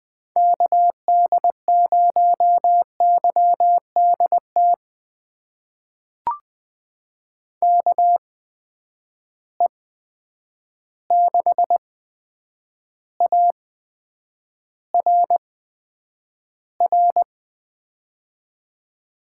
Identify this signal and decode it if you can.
Morse code, heard as a keyed tone, about 20 wpm, spaced as for 9 wpm, 700 Hz. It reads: KD0YDT KE6ARR
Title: USA Call Signs - 6 Characters - 8x Character Spacing - 20wpm